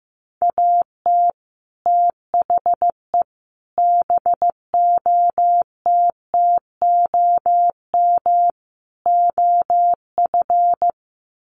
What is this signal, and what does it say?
Morse code, heard as a keyed tone, about 15 wpm, 700 Hz. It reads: AT THE BOTTOM OF
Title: Sets of 4 Words 15wpm